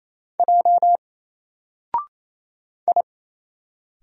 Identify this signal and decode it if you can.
Morse code, heard as a keyed tone, about 29 wpm, 700 Hz. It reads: J I
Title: Single Letter-Number 28wpm